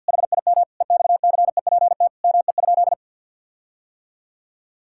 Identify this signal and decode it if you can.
Morse code, heard as a keyed tone, about 50 wpm, 700 Hz. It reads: HIM EXCEPT ME?
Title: Sentences from Top 1600 Words 50wpm